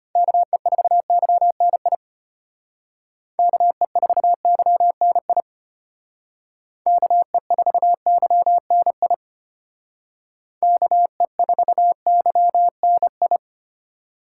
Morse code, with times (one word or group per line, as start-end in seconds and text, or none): 0.2-2.0 KE4YNI
3.4-5.4 KE4YNI
6.9-9.1 KE4YNI
10.6-13.4 KE4YNI